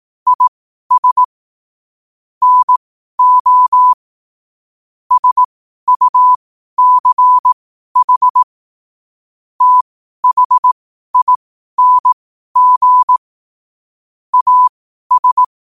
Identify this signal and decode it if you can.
Morse code, heard as a keyed tone, about 18 wpm, spaced as for 12 wpm, 1000 Hz. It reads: IS NO SUCH THING AS
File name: qotdfiles / 12wpm